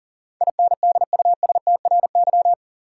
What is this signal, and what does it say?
Morse code, heard as a keyed tone, about 40 wpm, 700 Hz. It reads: INDUSTRY
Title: Sentences from Top 1300 Words - New Word Drill Alt - 40wpm